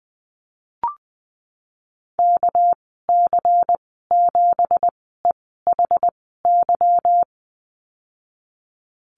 Morse code, heard as a keyed tone, about 20 wpm, spaced as for 14 wpm, 700 Hz. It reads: KC7EHY
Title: 245 Lesson 39g - Letter z - Call Signs 20wpm